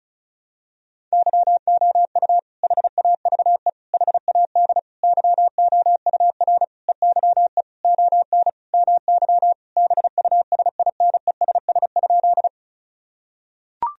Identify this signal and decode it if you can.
Morse code, heard as a keyed tone, about 36 wpm, 700 Hz. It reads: YOU HAVE HAD YOUR EYE ON MY BUSINESS?
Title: Sentences from Top 1000 Words - Review Part 1 - 35wpm